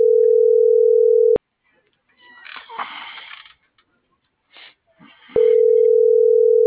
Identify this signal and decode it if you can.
Ringback tone (North America), 440 + 480 Hz.